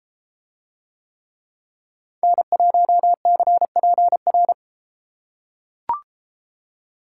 Morse code, heard as a keyed tone, about 31 wpm, 700 Hz. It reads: N1CPR